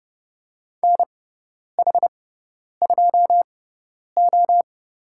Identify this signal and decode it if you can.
Morse code, heard as a keyed tone, about 30 wpm, spaced as for 10 wpm, 700 Hz. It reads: NH2O
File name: Lesson 16g - Letter c - Call Signs 30wpm